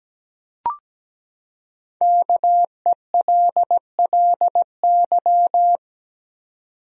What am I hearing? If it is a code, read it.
Morse code, heard as a keyed tone, about 17 wpm, 700 Hz. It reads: KELLY
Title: USA Common Names - 17wpm